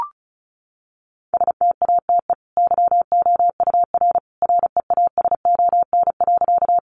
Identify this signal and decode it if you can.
Morse code, heard as a keyed tone, about 34 wpm, 700 Hz. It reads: STATE YOUR REASON.
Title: Sentences from Top 700 Words 35wpm